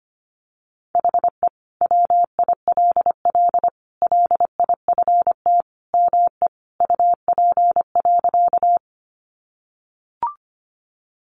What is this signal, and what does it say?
Morse code, heard as a keyed tone, about 25 wpm, 700 Hz. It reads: HE WILL LIFT ME UP.
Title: Sentences from Top 1000 Words 25wpm